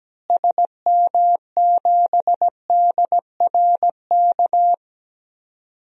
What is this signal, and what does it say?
Morse code, heard as a keyed tone, about 17 wpm, 700 Hz. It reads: SM7DRK
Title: DXCC-100 Call Signs - 6 Characters - Repeated 2x - 17wpm